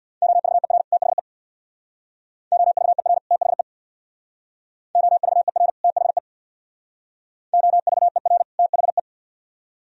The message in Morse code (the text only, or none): OVER THE OVER THE OVER THE OVER THE